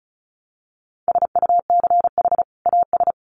Morse code, heard as a keyed tone, about 34 wpm, 700 Hz. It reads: SUCH AS
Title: Sets of 2 Words 35wpm